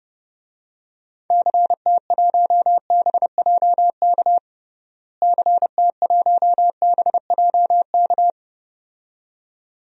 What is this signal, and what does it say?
Morse code, heard as a keyed tone, about 30 wpm, 700 Hz. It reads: CT1BJK CT1BJK
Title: DXCC-100 Call Signs - 6 Characters - Repeated 2x - 30wpm